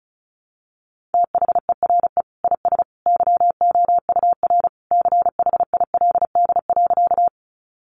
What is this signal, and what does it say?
Morse code, heard as a keyed tone, about 34 wpm, 700 Hz. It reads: THERE IS YOUR CHILD.
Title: Sentences from Top 800 Words 35wpm